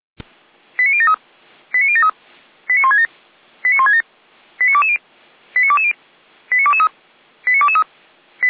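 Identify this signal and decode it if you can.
Five-tone selective call (ZVEI), heard as quick runs of five tones, tones of about 70 ms. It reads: 89073 89073 89167 89167 89220 89220 89223 89223
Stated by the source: ZVEI five-tone call 89123 received on an SDR, logged by the Feuerwehr Orlamünde (fire brigade) server.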